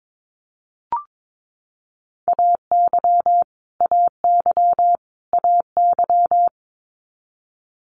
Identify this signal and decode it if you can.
Morse code, heard as a keyed tone, about 22 wpm, 700 Hz. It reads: AY AY AY
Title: Multi-letter Phonemes Repeated 3x 22wpm